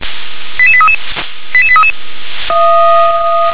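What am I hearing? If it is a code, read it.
Five-tone selective call (ZVEI), heard as quick runs of five tones, tones of about 70 ms. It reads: 88833 88833
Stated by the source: Five-tone ZVEI selective call received on an SDR.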